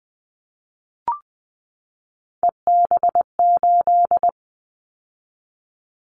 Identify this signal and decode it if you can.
Morse code, heard as a keyed tone, about 20 wpm, 700 Hz. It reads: EB8
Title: Callsign Prefixes for Top 100 DXCC - 20wpm